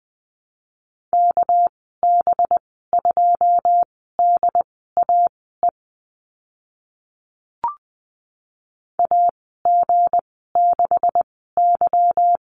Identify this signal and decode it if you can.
Morse code, heard as a keyed tone, about 20 wpm, spaced as for 15 wpm, 700 Hz. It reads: KB2DAE AG6Y
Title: 200 Lesson 32g - Letter k - Call Signs 20wpm